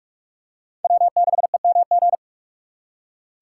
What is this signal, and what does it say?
Morse code, heard as a keyed tone, about 42 wpm, 700 Hz. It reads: W6EMG